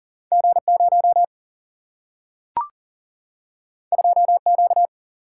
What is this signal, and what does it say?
Morse code, heard as a keyed tone, about 40 wpm, 700 Hz. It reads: G 0 2 Q
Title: Single-Letter Single-Number 40wpm